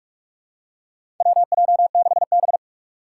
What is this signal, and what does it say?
Morse code, heard as a keyed tone, about 46 wpm, 700 Hz. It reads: WJ6B
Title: Four Challenging Letter-Number Combinations 45wpm